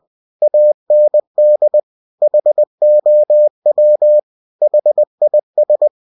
Morse code, heard as a keyed tone, about 20 wpm, 600 Hz. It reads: AND HOW HIS
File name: cw-books / winnie-the-pooh / winnie-20wpm-page-0022